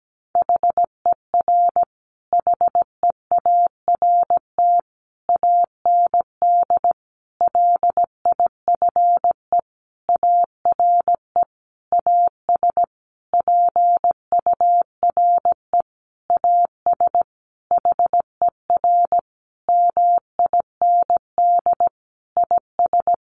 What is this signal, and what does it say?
Morse code, heard as a keyed tone, about 17 wpm, 700 Hz. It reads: HER HEART AND LIFE ARE AS PURE AS HER MIND IS
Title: Sentences from Top 1500 Words 17wpm